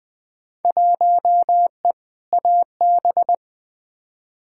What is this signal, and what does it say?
Morse code, heard as a keyed tone, about 20 wpm, 700 Hz. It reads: 1E AB
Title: Field Day Exchanges - 20wpm